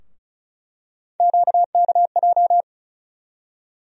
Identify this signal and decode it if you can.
Morse code, heard as a keyed tone, about 36 wpm, 700 Hz. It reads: QKJ
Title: USA Call Sign Postfixes - 3 Letters 35wpm